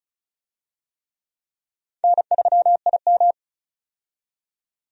Morse code, heard as a keyed tone, about 36 wpm, 700 Hz. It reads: N3IM
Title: USA Call Signs - 4 Characters - 35wpm